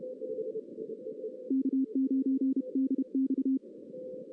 Morse code, heard as a keyed tone, about 31 wpm, 270 Hz. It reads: K9DX
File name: 2016 arrl dx cw / 2016-ARRL-DX-CW-K5TR- - 0-3